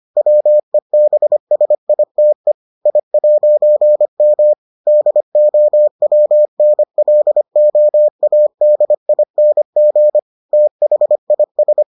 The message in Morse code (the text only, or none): WEBSITE I'M DOWNLOADING THIS